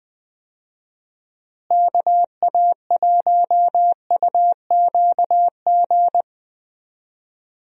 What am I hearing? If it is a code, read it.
Morse code, heard as a keyed tone, about 20 wpm, 700 Hz. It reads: KA1UQG
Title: USA Call Signs - 6 Characters - Repeated 3x - 20wpm